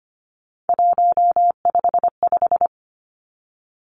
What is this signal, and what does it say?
Morse code, heard as a keyed tone, about 25 wpm, 700 Hz. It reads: 155
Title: Three Challenging Letter-Number Combinations 25wpm